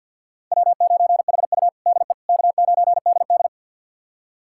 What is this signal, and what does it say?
Morse code, heard as a keyed tone, about 50 wpm, 700 Hz. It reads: W9HU DE K9DD
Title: USA Callsign DE Callsign - 4 Characters - 50wpm